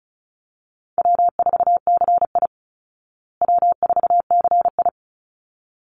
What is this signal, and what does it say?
Morse code, heard as a keyed tone, about 34 wpm, 700 Hz. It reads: W4CI W4CI
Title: Call Signs Repeated 3x - 35wpm